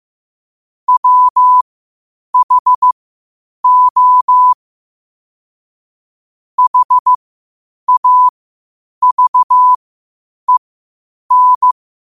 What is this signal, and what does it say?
Morse code, heard as a keyed tone, about 15 wpm, spaced as for 8 wpm, 1000 Hz. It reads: WHO HAVEN
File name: qotdfiles / 8wpm